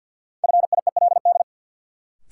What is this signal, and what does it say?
Morse code, heard as a keyed tone, about 53 wpm, 700 Hz. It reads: FIELD
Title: Sentences from Top 1000 Words - New Word Drill - 50wpm